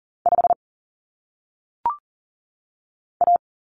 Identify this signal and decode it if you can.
Morse code, heard as a keyed tone, about 40 wpm, 700 Hz. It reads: 5 A
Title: Common Cut Numbers - 1-Digit 40wpm